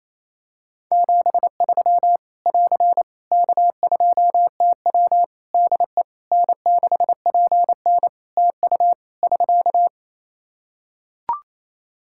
Morse code, heard as a keyed tone, about 28 wpm, 700 Hz. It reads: 73 <AR> K2TW DE N6PN TU <SK>
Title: Practice QSO - Exchange 4 - Ending - 28wpm